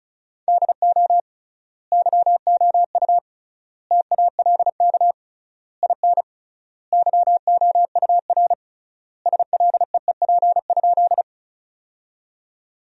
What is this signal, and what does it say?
Morse code, heard as a keyed tone, about 34 wpm, 700 Hz. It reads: DO YOU TALK IN YOUR SLEEP?
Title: 3x Word-Spacing - Sentences from Top 500 Words - 35wpm